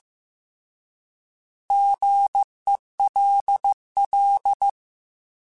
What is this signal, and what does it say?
Morse code, heard as a keyed tone, about 15 wpm, 800 Hz. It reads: GELL